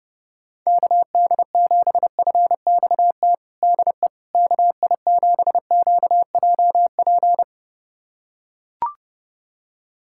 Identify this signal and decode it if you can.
Morse code, heard as a keyed tone, about 30 wpm, 700 Hz. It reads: KD7FXT DE KI7QJP
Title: USA Callsign DE Callsign - 6 Characters - 30wpm